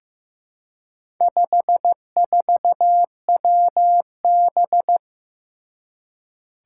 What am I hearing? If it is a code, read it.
Morse code, heard as a keyed tone, about 15 wpm, 700 Hz. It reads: 54WB
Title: Four Challenging Letter-Number Combinations 15wpm